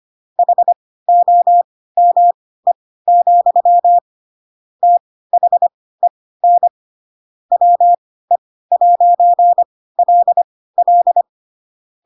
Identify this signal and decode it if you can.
Morse code, heard as a keyed tone, about 25 wpm, spaced as for 16 wpm, 700 Hz. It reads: HOME, THEN WE'LL
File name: Pinocchio / Pinocchio ch3 10 25 ik1hge